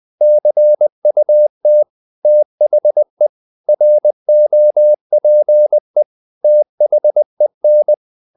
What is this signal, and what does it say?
Morse code, heard as a keyed tone, about 20 wpm, 600 Hz. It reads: CUT THE ROPE THEN
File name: the old man and the sea part 2 20wpm